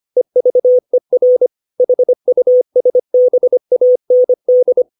Morse code, heard as a keyed tone, about 25 wpm, 500 Hz. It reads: EVER HUSBAND
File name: 632 words / 632 Words 25WPM